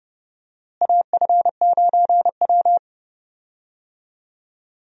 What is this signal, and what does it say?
Morse code, heard as a keyed tone, about 30 wpm, 700 Hz. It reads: AF9W
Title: USA Call Signs - 4 Characters - Repeated 3x - 30wpm